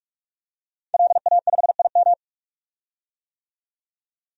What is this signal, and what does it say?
Morse code, heard as a keyed tone, about 42 wpm, 700 Hz. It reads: LA5IM
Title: DXCC-100 Call Signs - 5 Characters - 45wpm